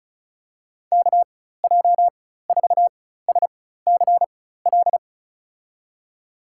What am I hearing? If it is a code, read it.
Morse code, heard as a keyed tone, about 36 wpm, spaced as for 16 wpm, 700 Hz. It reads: KJ4SCL
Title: USA Call Signs - 6 Characters - 4x Character Spacing - 35wpm